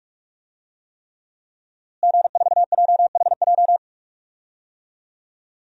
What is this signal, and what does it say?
Morse code, heard as a keyed tone, about 46 wpm, 700 Hz. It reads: G4JHJ
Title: DXCC-100 Call Signs - 5 Characters - Repeated 3x - 45wpm